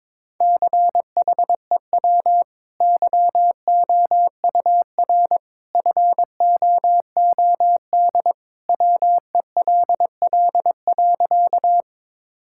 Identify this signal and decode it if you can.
Morse code, heard as a keyed tone, about 22 wpm, 700 Hz. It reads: CHEW YOUR FOOD WELL.
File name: practice / 22wpm / Easy Sentences with Hints - 22wpm